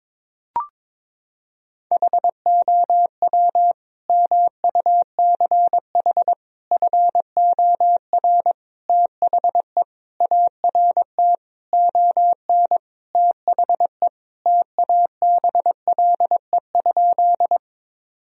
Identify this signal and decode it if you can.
Morse code, heard as a keyed tone, about 23 wpm, 700 Hz. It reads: HOW MUCH FOR THE ART ON THE TABLE?
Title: Sentences from Top 1000 Words - Review Part 3 - 22wpm